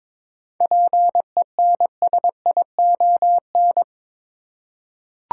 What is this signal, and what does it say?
Morse code, heard as a keyed tone, about 21 wpm, 700 Hz. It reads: PENSION
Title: Seven Letter Words - 22wpm